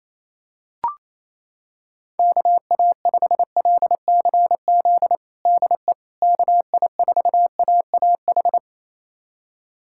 Morse code, heard as a keyed tone, about 28 wpm, 700 Hz. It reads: KA5LCZ DE KI4AAH
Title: USA Callsign DE Callsign - 6 Characters - 28wpm